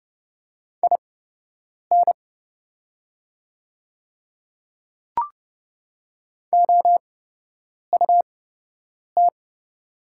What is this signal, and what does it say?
Morse code, heard as a keyed tone, about 30 wpm, spaced as for 8 wpm, 700 Hz. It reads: IN OUT
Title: 8x Character Spacing - Top 100 Words - 30wpm